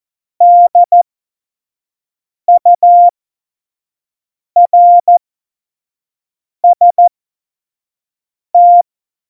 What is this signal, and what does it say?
Morse code, heard as a keyed tone, about 14 wpm, spaced as for 5 wpm, 700 Hz. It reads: DURST
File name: Morsekurs / Witze / Witz 07. November 2017, 25 F70